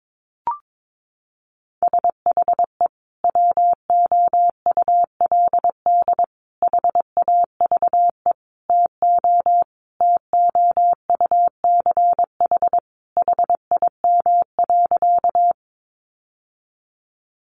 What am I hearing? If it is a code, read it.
Morse code, heard as a keyed tone, about 22 wpm, 700 Hz. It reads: SHE WOULD HAVE TO TOUCH HIM.
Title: Sentences from Top 1200 Words 22wpm